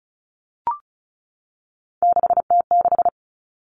Morse code, heard as a keyed tone, about 34 wpm, 700 Hz. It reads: 6T6
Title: All Cut Numbers - 3-Digit 35wpm